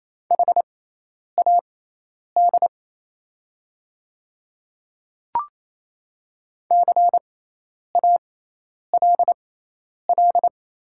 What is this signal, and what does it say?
Morse code, heard as a keyed tone, about 28 wpm, spaced as for 10 wpm, 690 Hz. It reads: HAD CALL
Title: 6x Character Spacing - Top 500 Words - 28wpm